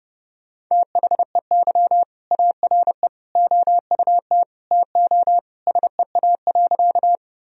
Morse code, heard as a keyed tone, about 30 wpm, 700 Hz. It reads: THEY ARE OUT TO SEA.
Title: Sentences from Top 1000 Words - Review Part 1 - 30wpm